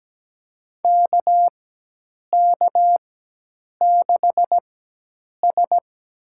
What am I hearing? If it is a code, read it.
Morse code, heard as a keyed tone, about 17 wpm, spaced as for 8 wpm, 700 Hz. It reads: KK6S
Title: USA Call Signs - 4x Character Spacing - 17wpm